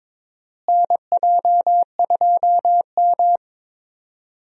Morse code, heard as a keyed tone, about 21 wpm, 700 Hz. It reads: NJ2M
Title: USA Call Signs - 4 Characters - Repeated 2x - 22wpm